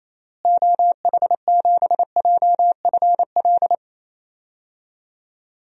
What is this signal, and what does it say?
Morse code, heard as a keyed tone, about 27 wpm, 700 Hz. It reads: OH7JFL
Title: DXCC-100 Call Signs - 6 Characters - Repeated 3x - 28wpm